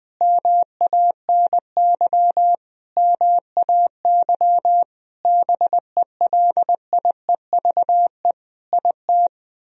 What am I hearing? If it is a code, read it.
Morse code, heard as a keyed tone, about 20 wpm, 700 Hz. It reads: MANY MAY BELIEVE IT
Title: Sentences from Top 1000 Words - Review Part 4 - 20wpm